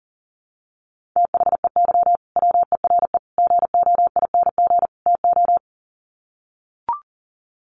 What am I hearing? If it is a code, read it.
Morse code, heard as a keyed tone, about 40 wpm, 700 Hz. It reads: THEY WERE GOING TO
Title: Sets of 4 Words 40wpm